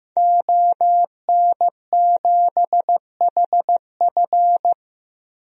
Morse code, heard as a keyed tone, about 16 wpm, 700 Hz. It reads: ON7HF
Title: DXCC-100 Call Signs - 5 Characters - Repeated 3x - 15wpm